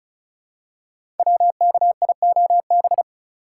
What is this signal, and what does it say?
Morse code, heard as a keyed tone, about 36 wpm, 700 Hz. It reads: WKIOB
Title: Code Groups - 5 Letters - 35wpm